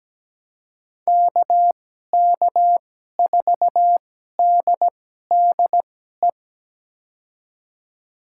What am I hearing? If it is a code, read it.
Morse code, heard as a keyed tone, about 17 wpm, spaced as for 12 wpm, 700 Hz. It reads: KK4DDE
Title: USA Call Signs - 6 Characters - 2x Character Spacing - 17wpm